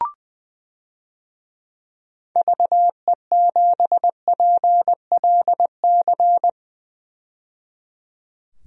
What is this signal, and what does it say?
Morse code, heard as a keyed tone, about 21 wpm, 700 Hz. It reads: VE7PLC
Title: DXCC-100 Call Signs - 6 Characters - 20wpm